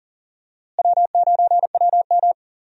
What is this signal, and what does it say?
Morse code, heard as a keyed tone, about 40 wpm, 700 Hz. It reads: W9WM
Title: USA Call Signs - 4 Characters - Repeated 3x - 40wpm